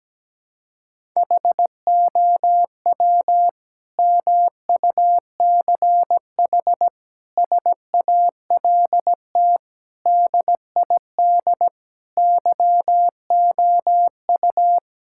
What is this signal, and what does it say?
Morse code, heard as a keyed tone, about 17 wpm, 700 Hz. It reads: HOW MUCH SALT DID YOU
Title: Sentences from Top 1000 Words - Review Part 5 - 17wpm